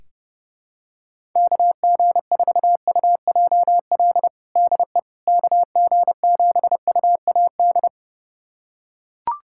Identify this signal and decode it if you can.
Morse code, heard as a keyed tone, about 32 wpm, 700 Hz. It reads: KG4UJL DE KG7UAD
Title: USA Callsign DE Callsign 30wpm